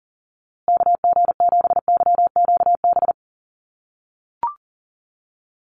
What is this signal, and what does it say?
Morse code, heard as a keyed tone, about 40 wpm, 700 Hz. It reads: KG7YQB
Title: USA Call Signs Repeated 2x 40wpm